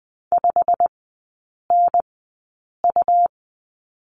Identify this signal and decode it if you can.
Morse code, heard as a keyed tone, about 20 wpm, spaced as for 8 wpm, 700 Hz. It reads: 5NU
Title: 100 Lesson 17e - Letter u - Three Characters 20wpm